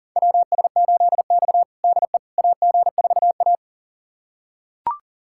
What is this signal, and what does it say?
Morse code, heard as a keyed tone, about 40 wpm, 700 Hz. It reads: WS8X DE AG4A
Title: USA Callsign DE Callsign - 4 Characters - 40wpm